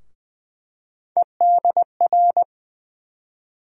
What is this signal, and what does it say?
Morse code, heard as a keyed tone, about 20 wpm, 700 Hz. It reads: EDR
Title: USA Call Sign Postfixes - 3 Letters 20wpm